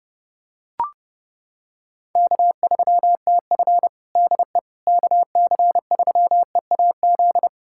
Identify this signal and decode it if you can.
Morse code, heard as a keyed tone, about 32 wpm, 700 Hz. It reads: K3TF DE KC3EAZ
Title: Practice QSO - Exchange 4 - Ending - 30wpm